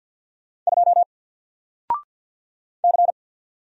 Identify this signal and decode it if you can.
Morse code, heard as a keyed tone, about 48 wpm, 700 Hz. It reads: 2 C
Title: Single Letter-Number 50wpm